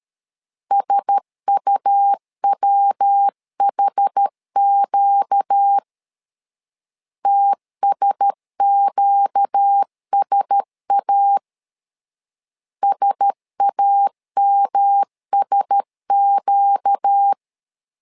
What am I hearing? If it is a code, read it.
Morse code, heard as a keyed tone, about 13 wpm, 800 Hz. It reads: SUWHQ TSQSA SAMSQ
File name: LEZIONI / 4 Q1S 12wpm 1802